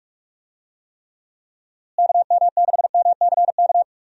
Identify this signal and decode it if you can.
Morse code, heard as a keyed tone, about 46 wpm, 700 Hz. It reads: KM6MCK
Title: USA Call Signs - 6 Characters - Repeated 2x - 45wpm